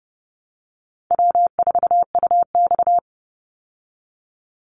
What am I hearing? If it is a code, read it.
Morse code, heard as a keyed tone, about 30 wpm, 700 Hz. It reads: W4UX
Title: Easy Call Signs Repeated 2x - 30wpm